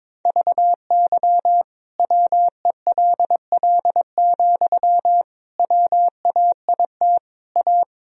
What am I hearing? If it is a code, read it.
Morse code, heard as a keyed tone, about 22 wpm, 700 Hz. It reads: VY WELL, WAIT A
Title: Sentences from Top 500 Words and Abbreviations 22wpm